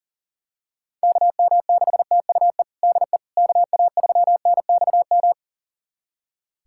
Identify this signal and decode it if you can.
Morse code, heard as a keyed tone, about 37 wpm, 700 Hz. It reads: KM6TUE DE KA3NXM
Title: USA Callsign DE Callsign 40wpm